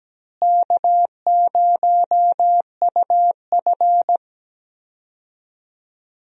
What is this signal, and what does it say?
Morse code, heard as a keyed tone, about 17 wpm, 700 Hz. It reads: K0UF
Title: USA Call Signs - 4 Characters - Repeated 3x - 17wpm